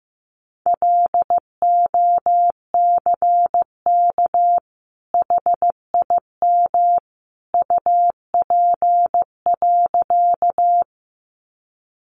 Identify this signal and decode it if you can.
Morse code, heard as a keyed tone, about 15 wpm, 700 Hz. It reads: LOCK HIM UP.
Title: Sentences from Top 1300 Words 15wpm